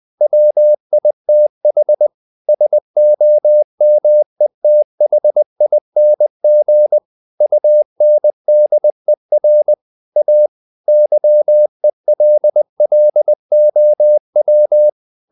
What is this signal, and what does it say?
Morse code, heard as a keyed tone, about 20 wpm, 600 Hz. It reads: WITH SOMETHING UNDER A YELLOW
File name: the old man and the sea part 2 20wpm